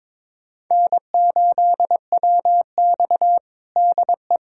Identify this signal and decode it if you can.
Morse code, heard as a keyed tone, about 21 wpm, 700 Hz. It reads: N8WX DE
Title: USA Callsign DE Callsign - 4 Characters - 22wpm